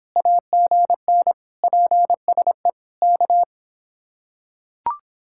Morse code, heard as a keyed tone, about 24 wpm, 700 Hz. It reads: AGN PSE K